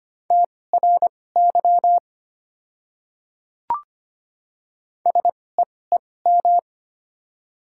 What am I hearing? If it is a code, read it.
Morse code, heard as a keyed tone, about 25 wpm, spaced as for 18 wpm, 700 Hz. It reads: TRY SEEM
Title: 2x Character Spacing - Top 300 Words - 25wpm